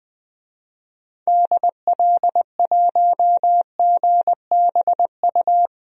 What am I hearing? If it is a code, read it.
Morse code, heard as a keyed tone, about 20 wpm, 700 Hz. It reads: DL1GBU
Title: DXCC-100 Call Signs - 6 Characters - 20wpm